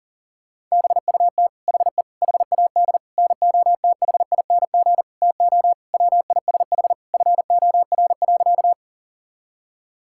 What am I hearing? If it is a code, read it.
Morse code, heard as a keyed tone, about 40 wpm, 700 Hz. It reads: BUT HE HAD NOTHING TO WISH FOR.
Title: Sentences from Top 1600 Words 40wpm